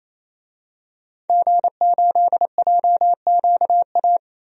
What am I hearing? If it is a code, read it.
Morse code, heard as a keyed tone, about 28 wpm, 700 Hz. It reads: G8JQA
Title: DXCC-100 Call Signs - 5 Characters - Repeated 3x - 28wpm